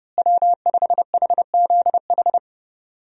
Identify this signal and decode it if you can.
Morse code, heard as a keyed tone, about 30 wpm, 700 Hz. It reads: W5HZH
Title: USA Call Signs - 5 Characters - Repeated 3x - 30wpm